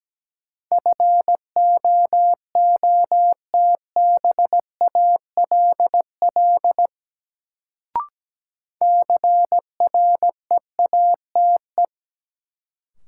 Morse code, heard as a keyed tone, about 17 wpm, 700 Hz. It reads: FOOTBALL CREATE
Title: Sentences from Top 1600 Words - New Word Drill Alt - 17wpm